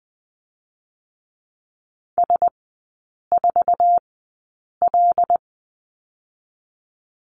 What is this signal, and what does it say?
Morse code, heard as a keyed tone, about 20 wpm, spaced as for 8 wpm, 700 Hz. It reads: S4L
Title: 087 Lesson 15e - Number 5 - Three Characters 20wpm